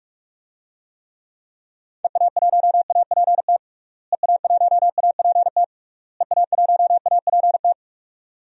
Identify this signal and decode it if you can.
Morse code, heard as a keyed tone, about 48 wpm, 700 Hz. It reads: EA1APT EA1APT EA1APT
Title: DXCC-100 Call Signs - 6 Characters - Repeated 3x - 45wpm